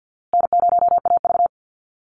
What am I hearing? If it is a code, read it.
Morse code, heard as a keyed tone, about 50 wpm, 700 Hz. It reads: N0AV
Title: Easy Call Signs Repeated 2x - 50wpm